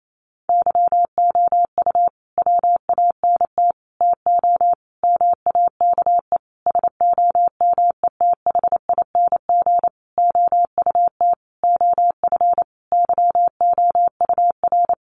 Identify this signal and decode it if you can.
Morse code, heard as a keyed tone, about 28 wpm, 700 Hz. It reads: YOU WANT TO MAKE SOMETHING OUT OF YOUR
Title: Sentences from Top 600 Words 28wpm